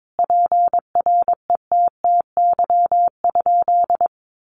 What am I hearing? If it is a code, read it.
Morse code, heard as a keyed tone, about 22 wpm, 700 Hz. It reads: PRETTY?
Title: Sentences from Top 1000 Words 22wpm